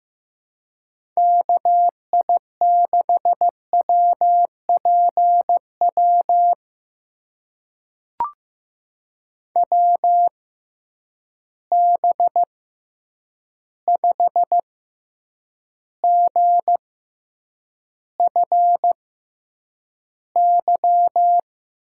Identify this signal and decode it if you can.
Morse code, heard as a keyed tone, about 15 wpm, spaced as for 8 wpm, 700 Hz. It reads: KI6WPW WB5GFY
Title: USA Call Signs - 6 Characters - 6x Character Spacing - 15wpm